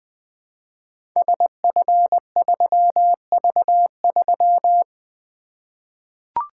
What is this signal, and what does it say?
Morse code, heard as a keyed tone, about 20 wpm, 700 Hz. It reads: SF3V3
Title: Code Groups - 5 Letter:Numbers - 20wpm